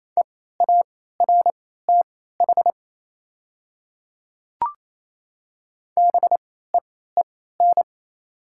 Morse code, heard as a keyed tone, about 28 wpm, spaced as for 16 wpm, 700 Hz. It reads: EARTH BEEN
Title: 3x Character Spacing - Top 300 Words - 28wpm